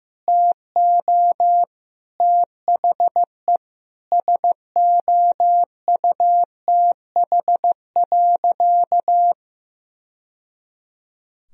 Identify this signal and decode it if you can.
Morse code, heard as a keyed tone, about 16 wpm, 700 Hz. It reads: TO THE SOUTH.
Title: Sentences from Top 1000 Words - Review Part 2 - 15wpm